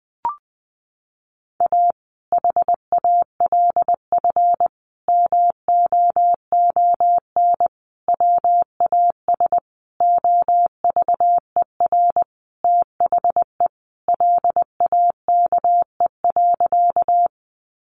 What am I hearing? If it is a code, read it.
Morse code, heard as a keyed tone, about 20 wpm, 700 Hz. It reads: A HALF MOON WAS OVER THE LAKE.
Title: Sentences from Top 1100 Words 20wpm